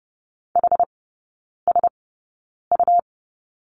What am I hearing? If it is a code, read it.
Morse code, heard as a keyed tone, about 30 wpm, spaced as for 9 wpm, 700 Hz. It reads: HSU
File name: Lesson 17e - Letter u - Three Characters 30wpm